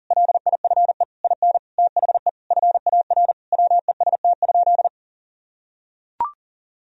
Morse code, heard as a keyed tone, about 40 wpm, 700 Hz. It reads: LIFE IN THE FAR WEST?
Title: Sentences from Top 1000 Words - Review Part 2 - 40wpm